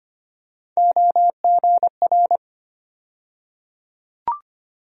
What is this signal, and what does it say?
Morse code, heard as a keyed tone, about 25 wpm, 700 Hz. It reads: OGR
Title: USA Call Sign Postfixes - 3 Letters 25wpm